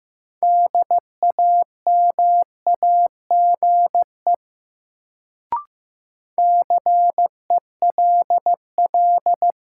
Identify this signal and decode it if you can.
Morse code, heard as a keyed tone, about 15 wpm, 700 Hz. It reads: DAMAGE CELL
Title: Sentences from Top 1100 Words - New Word Drill - 15wpm